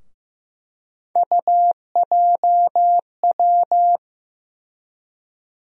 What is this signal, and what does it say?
Morse code, heard as a keyed tone, about 15 wpm, 700 Hz. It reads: UJW
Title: USA Call Sign Postfixes - 3 Letters 15wpm